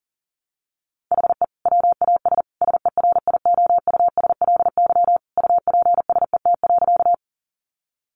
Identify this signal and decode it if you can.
Morse code, heard as a keyed tone, about 40 wpm, 700 Hz. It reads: HE WAS SERIOUSLY UPSET.
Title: Sentences from Top 1200 Words 40wpm